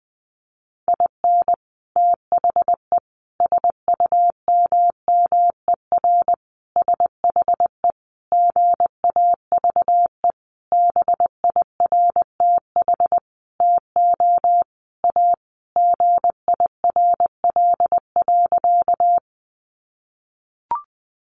Morse code, heard as a keyed tone, about 20 wpm, 700 Hz. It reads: IN THE SUMMER SHE GAVE BIRTH TO A GIRL.
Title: Sentences from Top 1100 Words 20wpm